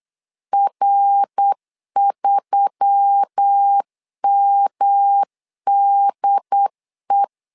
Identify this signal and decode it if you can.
Morse code, heard as a keyed tone, about 8 wpm, 800 Hz. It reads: R3MDE
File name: LEZIONI / 4 Q1S3 8wpm 3037